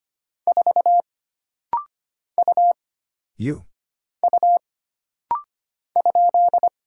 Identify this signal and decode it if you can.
Morse code, heard as a keyed tone, about 25 wpm, 700 Hz. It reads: 4 U U ?
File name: Single Character (Letter:Number:Punctuation:Prosign) - Rapid-Fire - 25wpm